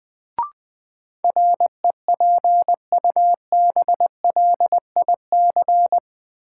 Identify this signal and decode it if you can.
Morse code, heard as a keyed tone, about 20 wpm, 700 Hz. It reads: REPUBLIC
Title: Sentences from Top 1500 Words - New Word Drill - 20wpm